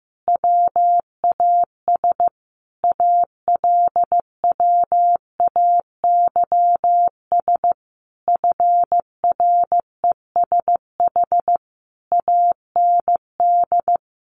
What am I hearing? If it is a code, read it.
Morse code, heard as a keyed tone, about 15 wpm, 700 Hz. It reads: WAS ALWAYS FRESH AND
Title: Sentences from Top 1200 Words 15wpm